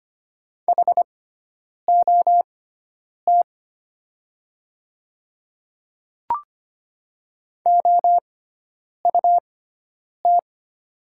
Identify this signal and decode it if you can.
Morse code, heard as a keyed tone, about 25 wpm, spaced as for 9 wpm, 700 Hz. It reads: HOT OUT
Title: 6x Character Spacing - Top 500 Words - 25wpm